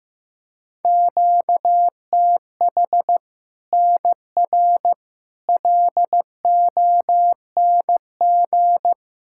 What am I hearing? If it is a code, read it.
Morse code, heard as a keyed tone, about 15 wpm, 700 Hz. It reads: QTH NR LONG
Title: K1USN SST QSO Practice 15wpm